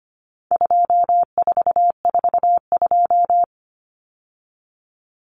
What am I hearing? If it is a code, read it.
Morse code, heard as a keyed tone, about 25 wpm, 700 Hz. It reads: 2442
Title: Numbers - 4-Digit 25wpm